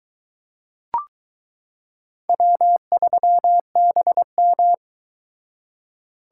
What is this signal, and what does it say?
Morse code, heard as a keyed tone, about 22 wpm, 700 Hz. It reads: W3BM